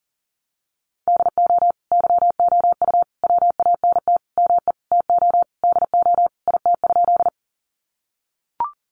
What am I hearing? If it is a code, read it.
Morse code, heard as a keyed tone, about 40 wpm, 700 Hz. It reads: DO YOU WANT ME TO DO IT?
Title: Sentences from Top 200 Words 40wpm